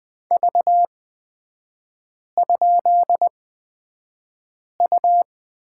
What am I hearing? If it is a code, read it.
Morse code, heard as a keyed tone, about 20 wpm, spaced as for 5 wpm, 700 Hz. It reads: V?U
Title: 224 Lesson 36e - BT - Three Characters 20wpm